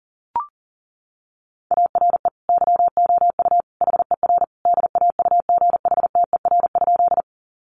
Morse code, heard as a keyed tone, about 37 wpm, 700 Hz. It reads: ARE YOU HER DAUGHTER?
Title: Sentences from Top 1500 Words 40wpm